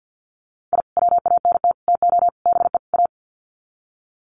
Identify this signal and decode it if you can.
Morse code, heard as a keyed tone, about 51 wpm, 700 Hz. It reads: I WANT TO BE A
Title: Sets of 5 Words Encore 50wpm